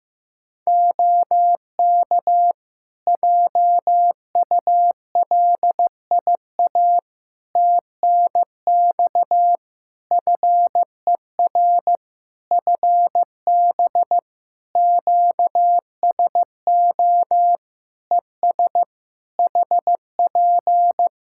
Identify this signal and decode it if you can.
Morse code, heard as a keyed tone, about 15 wpm, 700 Hz. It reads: OK JULIA TNX FER FB QSO ES HP
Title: Practice QSO - Exchange 4 - Ending - 15wpm